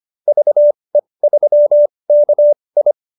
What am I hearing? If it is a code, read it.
Morse code, heard as a keyed tone, about 25 wpm, spaced as for 20 wpm, 590 Hz. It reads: VE3KI